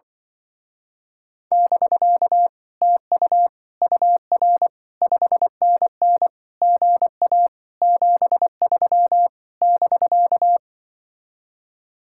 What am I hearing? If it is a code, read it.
Morse code, heard as a keyed tone, about 24 wpm, 700 Hz. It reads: <BK> TU UR 5NN GA 73 <BK>
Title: Speed-Racing Common QSO Phrases Encore - 20wpm